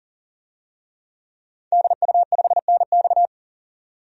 Morse code, heard as a keyed tone, about 37 wpm, 700 Hz. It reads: DU5NX
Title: DXCC-100 Call Signs - 5 Characters - 40wpm